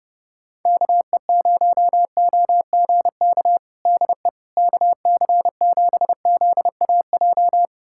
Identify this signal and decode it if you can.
Morse code, heard as a keyed tone, about 32 wpm, 700 Hz. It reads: KE0OGK DE KC7ZAJ
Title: USA Callsign DE Callsign - 6 Characters - 30wpm